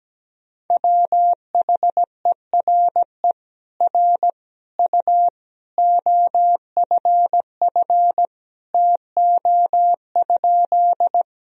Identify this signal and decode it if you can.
Morse code, heard as a keyed tone, about 17 wpm, 700 Hz. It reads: WHERE R U OFF TO?
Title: Sentences from Top 200 Words and Abbreviations 17wpm